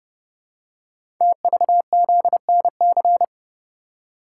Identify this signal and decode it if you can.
Morse code, heard as a keyed tone, about 30 wpm, 700 Hz. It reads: TVZNC
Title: Code Groups - 5 Letter:Numbers - 30wpm